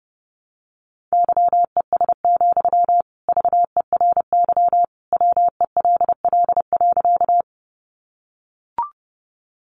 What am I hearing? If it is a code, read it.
Morse code, heard as a keyed tone, about 30 wpm, 700 Hz. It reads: YES, VERY WELL.
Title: Sentences from Top 500 Words 30wpm